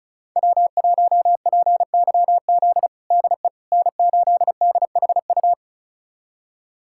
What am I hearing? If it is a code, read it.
Morse code, heard as a keyed tone, about 36 wpm, 700 Hz. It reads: W1PYZ DE N8DHU
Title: USA Callsign DE Callsign - 5 Characters - 35wpm